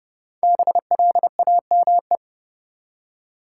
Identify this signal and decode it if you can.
Morse code, heard as a keyed tone, about 30 wpm, 700 Hz. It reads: BLAME
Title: Five Letter Words - 30wpm